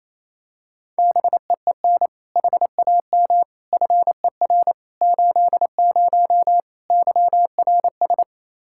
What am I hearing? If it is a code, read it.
Morse code, heard as a keyed tone, about 27 wpm, 700 Hz. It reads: BEEN HAM FER 80 YRS
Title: K1USN SST QSO Practice 28wpm